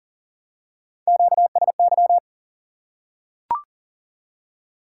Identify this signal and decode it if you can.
Morse code, heard as a keyed tone, about 41 wpm, 700 Hz. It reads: QSY
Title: Top 180 QSO Elements 40wpm